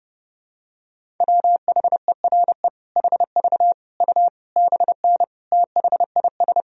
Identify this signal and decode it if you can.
Morse code, heard as a keyed tone, about 30 wpm, 700 Hz. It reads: WHERE HV U BN THIS
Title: Sentences from Top 300 Words and Abbreviations 30wpm